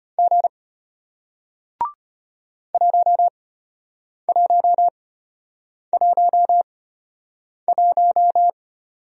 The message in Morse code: G 1111

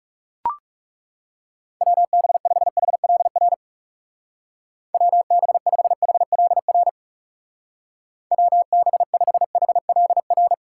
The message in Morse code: WB5HLR WB5HLR WB5HLR

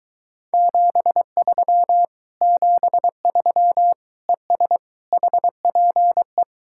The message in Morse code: 73 73 ES HPE

